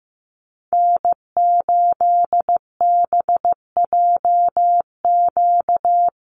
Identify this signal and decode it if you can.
Morse code, heard as a keyed tone, about 15 wpm, 700 Hz. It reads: N8BJQ
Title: Call Signs Repeated 3x - 15wpm